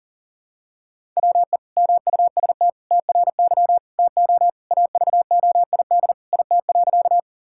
Morse code, heard as a keyed tone, about 40 wpm, 700 Hz. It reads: WE MUST TRY TO AVOID IT.
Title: Sentences from Top 1000 Words - Review Part 6 - 40wpm